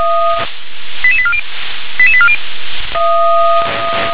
Five-tone selective call (ZVEI), heard as quick runs of five tones, tones of about 70 ms. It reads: 88840 88840